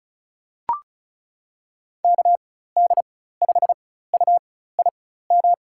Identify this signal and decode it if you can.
Morse code, heard as a keyed tone, about 36 wpm, spaced as for 16 wpm, 700 Hz. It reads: KD5UIM
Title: USA Call Signs - 6 Characters - 4x Character Spacing - 35wpm